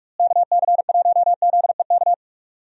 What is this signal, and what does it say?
Morse code, heard as a keyed tone, about 46 wpm, 700 Hz. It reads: KC1ZEK
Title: USA Call Signs - 6 Characters - Repeated 3x - 45wpm